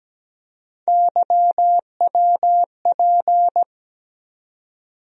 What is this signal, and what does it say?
Morse code, heard as a keyed tone, about 17 wpm, 700 Hz. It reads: YWP
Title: USA Call Sign Postfixes - 3 Letters 17wpm